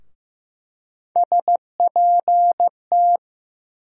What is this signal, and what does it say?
Morse code, heard as a keyed tone, about 15 wpm, 700 Hz. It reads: SPT
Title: USA Call Sign Postfixes - 3 Letters 15wpm